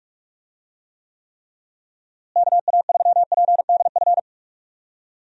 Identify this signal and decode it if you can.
Morse code, heard as a keyed tone, about 46 wpm, 700 Hz. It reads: KA3PDF
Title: USA Call Signs - 6 Characters - Repeated 3x - 45wpm